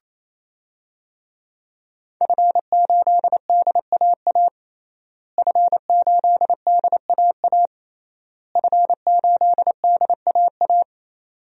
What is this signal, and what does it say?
Morse code, heard as a keyed tone, about 28 wpm, 700 Hz. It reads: F8DAA F8DAA F8DAA
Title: DXCC-100 Call Signs - 5 Characters - Repeated 3x - 28wpm